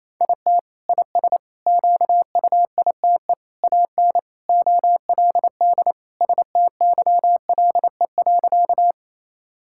Morse code, heard as a keyed tone, about 27 wpm, 700 Hz. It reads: IT IS QUITE AN OLD STYLE.
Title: Sentences from Top 1000 Words - Review Part 5 - 28wpm